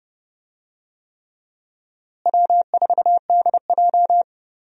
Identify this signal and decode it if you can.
Morse code, heard as a keyed tone, about 32 wpm, 700 Hz. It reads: W4DJ
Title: USA Call Signs - 4 Characters - Repeated 3x - 30wpm